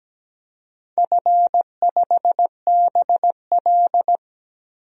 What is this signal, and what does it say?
Morse code, heard as a keyed tone, about 17 wpm, 700 Hz. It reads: F5BL
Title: DXCC-100 Call Signs - 4 Characters - 17wpm